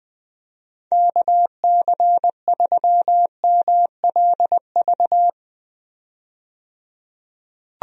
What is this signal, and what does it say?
Morse code, heard as a keyed tone, about 20 wpm, 700 Hz. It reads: KC3MLV
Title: USA Call Signs - 6 Characters - 20wpm